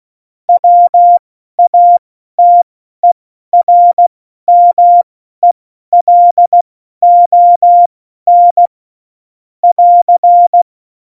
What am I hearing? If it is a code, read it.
Morse code, heard as a keyed tone, about 16 wpm, spaced as for 12 wpm, 700 Hz. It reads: WATERMELON <AR>
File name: Morsekurs / Witze